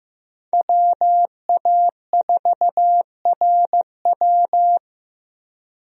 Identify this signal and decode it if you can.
Morse code, heard as a keyed tone, about 15 wpm, 700 Hz. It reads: WA4RW
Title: USA Call Signs - 5 Characters - 15wpm